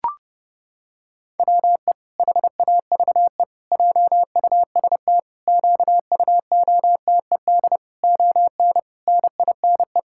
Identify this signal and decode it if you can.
Morse code, heard as a keyed tone, about 30 wpm, 700 Hz. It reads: WE HAVE JUST QUOTED ON NINE
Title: Pangram Challenge v2 - 30wpm